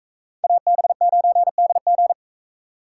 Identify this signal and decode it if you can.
Morse code, heard as a keyed tone, about 40 wpm, 700 Hz. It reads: AB9DG